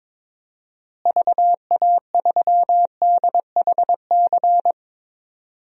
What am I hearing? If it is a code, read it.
Morse code, heard as a keyed tone, about 21 wpm, 700 Hz. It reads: VA3DHC
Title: DXCC-100 Call Signs - 6 Characters - Repeated 2x - 22wpm